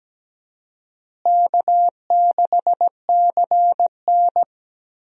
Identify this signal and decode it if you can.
Morse code, heard as a keyed tone, about 17 wpm, 700 Hz. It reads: K6CN
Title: USA Call Signs - 4 Characters - Repeated 3x - 17wpm